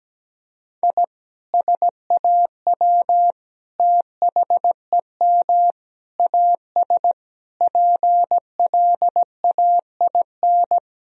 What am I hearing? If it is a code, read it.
Morse code, heard as a keyed tone, about 17 wpm, 700 Hz. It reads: I SAW THEM AS PLAIN
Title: Sentences from Top 1000 Words - Review Part 2 - 17wpm